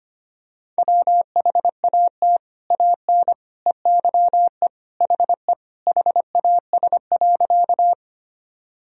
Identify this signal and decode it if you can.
Morse code, heard as a keyed tone, about 25 wpm, 700 Hz. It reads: WHAT AN EYE HE HAS.
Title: Sentences from Top 1000 Words - Review Part 1 - 25wpm